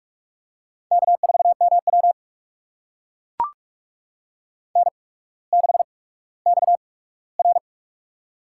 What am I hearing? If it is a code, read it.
Morse code, heard as a keyed tone, about 44 wpm, spaced as for 19 wpm, 700 Hz. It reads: K4MW N6XR
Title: USA Call Signs - 8x Character Spacing - 45wpm